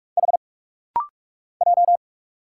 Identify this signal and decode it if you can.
Morse code, heard as a keyed tone, about 46 wpm, 700 Hz. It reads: H J
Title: Single Letter-Number - Rapid-Fire 45wpm